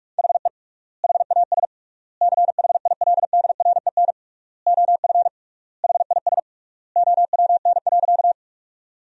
Morse code, heard as a keyed tone, about 46 wpm, 700 Hz. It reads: HE HAS CHILDREN OF HIS OWN.
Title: 3x Word-Spacing - Sentences from Top 300 Words - 45wpm